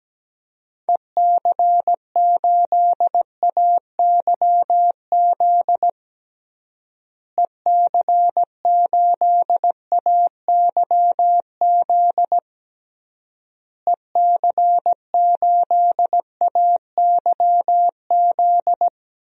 Morse code, heard as a keyed tone, about 17 wpm, 700 Hz. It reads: EC8AYZ EC8AYZ EC8AYZ
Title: DXCC-100 Call Signs - 6 Characters - Repeated 3x - 17wpm